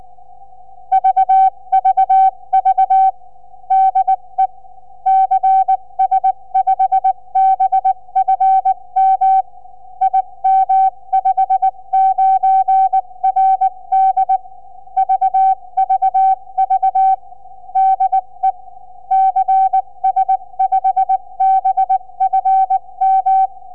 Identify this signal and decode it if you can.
Morse code, heard as a keyed tone, about 19 wpm, 770 Hz. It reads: VVV DE CS5BFM IM59RD VVV DE CS5BFM
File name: cs5bfm 4m